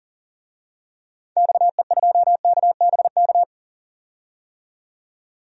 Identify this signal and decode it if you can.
Morse code, heard as a keyed tone, about 40 wpm, 700 Hz. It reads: XE2KBK
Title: DXCC-100 Call Signs - 6 Characters - 40wpm